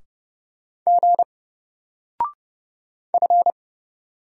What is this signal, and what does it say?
Morse code, heard as a keyed tone, about 30 wpm, 700 Hz. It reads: G F
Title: Single Letter-Number 30wpm